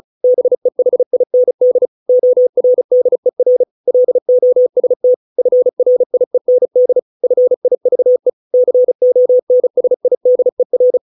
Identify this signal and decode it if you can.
Morse code, heard as a keyed tone, about 34 wpm, 500 Hz. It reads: BEHIND ORDER LOST FRIEND FIVE CONSIDER